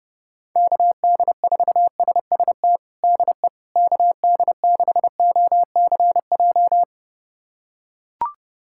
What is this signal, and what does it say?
Morse code, heard as a keyed tone, about 32 wpm, 700 Hz. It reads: KD4SST DE KD6OCJ
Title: USA Callsign DE Callsign - 6 Characters - 30wpm